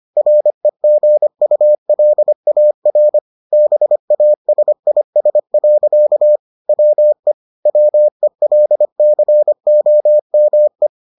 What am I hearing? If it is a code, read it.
Morse code, heard as a keyed tone, about 25 wpm, 600 Hz. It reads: REGULAR BASIS. WE WELCOME